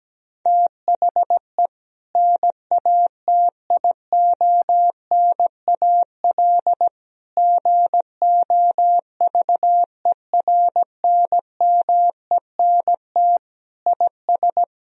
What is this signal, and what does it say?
Morse code, heard as a keyed tone, about 17 wpm, 700 Hz. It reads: THE NATIONAL GOVERNMENT IS
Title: Sentences from Top 1000 Words - Review Part 4 - 17wpm